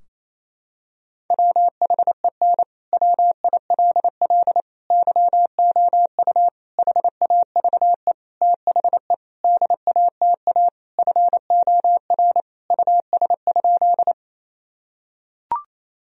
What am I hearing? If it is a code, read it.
Morse code, heard as a keyed tone, about 28 wpm, 700 Hz. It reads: WHEN WILL YOU HAVE THE DATA FOR US?
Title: Sentences from Top 1000 Words - Review Part 3 - 28wpm